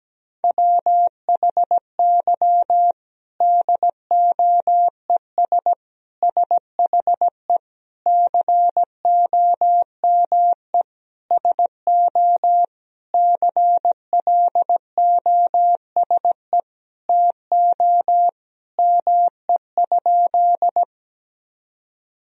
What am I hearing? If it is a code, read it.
Morse code, heard as a keyed tone, about 17 wpm, 700 Hz. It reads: WHY DOES SHE COME SO CLOSE TO ME?
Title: Sentences from Top 1000 Words - Review Part 1 - 17wpm